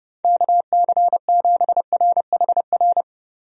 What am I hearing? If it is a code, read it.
Morse code, heard as a keyed tone, about 30 wpm, 700 Hz. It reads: KC7RHR
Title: USA Call Signs - 6 Characters - Repeated 3x - 30wpm